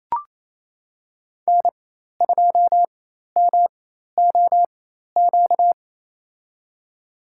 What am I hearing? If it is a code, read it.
Morse code, heard as a keyed tone, about 28 wpm, spaced as for 13 wpm, 700 Hz. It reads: N2MOQ
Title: USA Call Signs - 4x Character Spacing - 28wpm